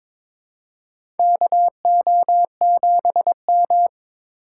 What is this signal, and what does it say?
Morse code, heard as a keyed tone, about 22 wpm, 700 Hz. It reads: KO7M
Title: USA Call Signs - 4 Characters - 3x Character Spacing - 22wpm